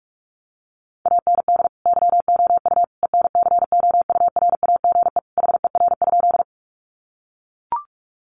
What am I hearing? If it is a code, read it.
Morse code, heard as a keyed tone, about 44 wpm, 700 Hz. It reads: AND YOU ENCOURAGE HER?
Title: Sentences from Top 1400 Words 45wpm